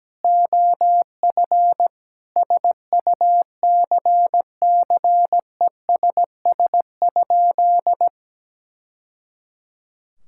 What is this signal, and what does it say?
Morse code, heard as a keyed tone, about 17 wpm, 700 Hz. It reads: OF SUCCESS?
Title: Sentences from Top 1000 Words - Review Part 6 - 17wpm